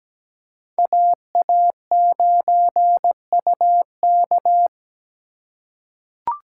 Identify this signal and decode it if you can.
Morse code, heard as a keyed tone, about 17 wpm, 700 Hz. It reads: AA9UK
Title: USA Call Signs - 3x Character Spacing - 17wpm